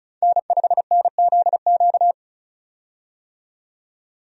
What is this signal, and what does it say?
Morse code, heard as a keyed tone, about 34 wpm, 700 Hz. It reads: N5NZQ
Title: USA Call Signs - 5 Characters - 35wpm